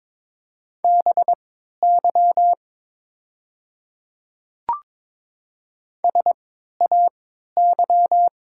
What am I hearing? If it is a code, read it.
Morse code, heard as a keyed tone, about 22 wpm, spaced as for 12 wpm, 690 Hz. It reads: BY SAY
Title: 3x Character Spacing - Top 300 Words - 22wpm